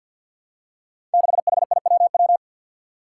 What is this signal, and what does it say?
Morse code, heard as a keyed tone, about 52 wpm, 700 Hz. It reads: 6HIWW
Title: Five Challenging Letter-Number Combinations 50wpm